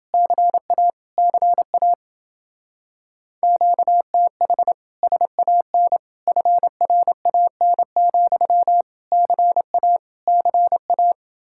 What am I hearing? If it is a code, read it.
Morse code, heard as a keyed tone, about 28 wpm, 700 Hz. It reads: CA CA QTH SAN FRAN, CA CA